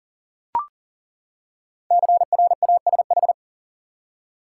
Morse code, heard as a keyed tone, about 39 wpm, 700 Hz. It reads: CRASH
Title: Five Letter Words - 40wpm